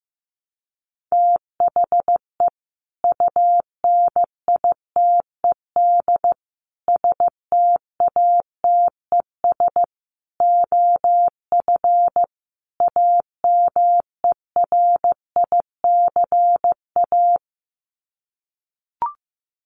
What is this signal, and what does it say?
Morse code, heard as a keyed tone, about 15 wpm, 700 Hz. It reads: THE UNITED STATES OF AMERICA
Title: Sets of 5 Words 15wpm